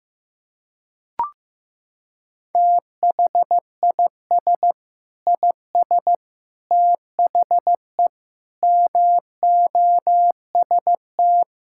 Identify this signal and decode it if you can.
Morse code, heard as a keyed tone, about 16 wpm, 700 Hz. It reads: THIS IS THE MOST
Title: Sentences from Top 1000 Words - Review Part 6 - 15wpm